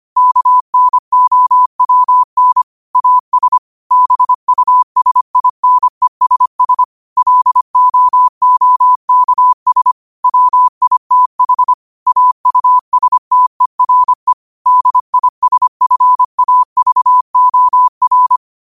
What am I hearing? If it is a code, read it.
Morse code, heard as a keyed tone, about 25 wpm, 1000 Hz. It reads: KNOWN AS BUSINESS LOOKS WITH AUSTERE DISFAVOR
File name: qotdfiles / 25wpm